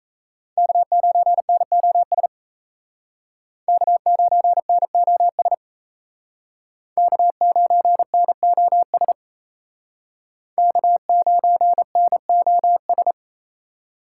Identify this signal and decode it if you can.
Morse code, heard as a keyed tone, about 35 wpm, 700 Hz. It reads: K9NOS K9NOS K9NOS K9NOS